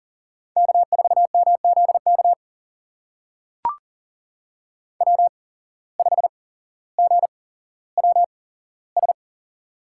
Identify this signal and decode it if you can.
Morse code, heard as a keyed tone, about 40 wpm, spaced as for 17 wpm, 700 Hz. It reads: K4MZK W5GWS
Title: USA Call Signs - 5 Characters - 8x Character Spacing - 40wpm